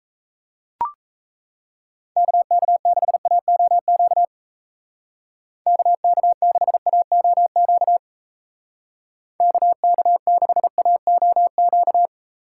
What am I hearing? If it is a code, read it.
Morse code, heard as a keyed tone, about 37 wpm, 700 Hz. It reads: KK6AOQ KK6AOQ KK6AOQ